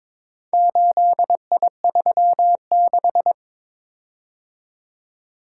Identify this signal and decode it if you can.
Morse code, heard as a keyed tone, about 22 wpm, 700 Hz. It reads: 8I36
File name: practice / 22wpm / Four Letter-Number Combinations - 22wpm